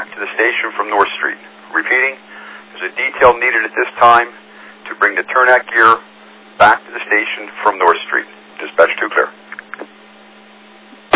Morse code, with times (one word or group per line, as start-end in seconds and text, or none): none